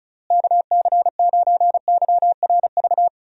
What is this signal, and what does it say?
Morse code, heard as a keyed tone, about 36 wpm, 700 Hz. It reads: KC9YRV
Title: USA Call Signs - 6 Characters - Repeated 2x - 35wpm